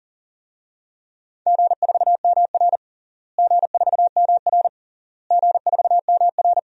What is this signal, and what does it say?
Morse code, heard as a keyed tone, about 40 wpm, 700 Hz. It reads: G4MR G4MR G4MR
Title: DXCC-100 Call Signs - 4 Characters - Repeated 3x - 40wpm